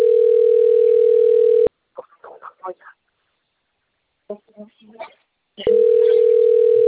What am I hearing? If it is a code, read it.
Ringback tone (North America), 440 + 480 Hz, with speech.
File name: out- - - - - .17018